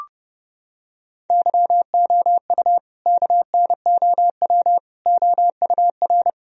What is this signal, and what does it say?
Morse code, heard as a keyed tone, about 30 wpm, 700 Hz. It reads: YOU KNOW OUR
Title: Sentences from Top 1000 Words - Review Part 4 - 30wpm